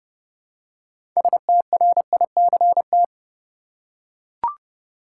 Morse code, heard as a keyed tone, about 30 wpm, 700 Hz. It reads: STRICT
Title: Six Letter Words - 30wpm